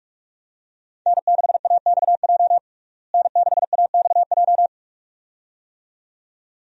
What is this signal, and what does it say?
Morse code, heard as a keyed tone, about 48 wpm, 700 Hz. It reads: N6AXJ N6AXJ
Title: USA Call Signs - 5 Characters - Repeated 2x - 45wpm